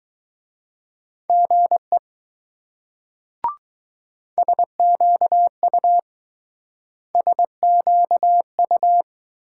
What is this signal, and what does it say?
Morse code, heard as a keyed tone, about 20 wpm, 700 Hz. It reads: GE SQU SQU